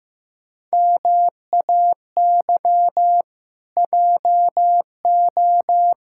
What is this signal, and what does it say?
Morse code, heard as a keyed tone, about 15 wpm, 700 Hz. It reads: MAY JO
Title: Pangram Challenge v2 - 15wpm